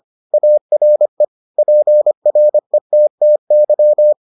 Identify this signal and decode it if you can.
Morse code, heard as a keyed tone, about 25 wpm, 600 Hz. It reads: ARE PRETTY